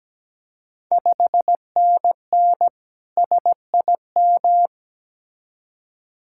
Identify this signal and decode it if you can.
Morse code, heard as a keyed tone, about 17 wpm, 690 Hz. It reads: 5NN SIM
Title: Ontario QSO Party Exchange 17wpm